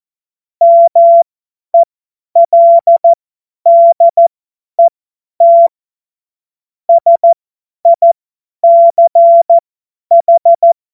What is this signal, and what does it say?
Morse code, heard as a keyed tone, about 14 wpm, spaced as for 10 wpm, 680 Hz. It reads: MELDET SICH
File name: Morsekurs / Witze / Witz 05. Oktober 2021, 50 F70